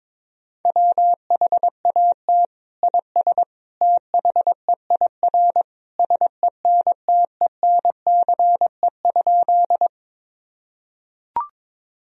Morse code, heard as a keyed tone, about 22 wpm, 700 Hz. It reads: WHAT IS THEIR SENTENCE?
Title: Sentences from Top 1000 Words - Review Part 1 - 22wpm